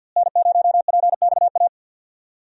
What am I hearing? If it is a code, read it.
Morse code, heard as a keyed tone, about 51 wpm, 700 Hz. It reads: N0PXA
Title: USA Call Signs - 5 Characters - Repeated 3x - 50wpm